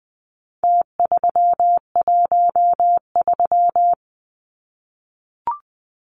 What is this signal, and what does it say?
Morse code, heard as a keyed tone, about 20 wpm, 700 Hz. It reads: T313
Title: Numbers - Contest Style - 4-Digit 20wpm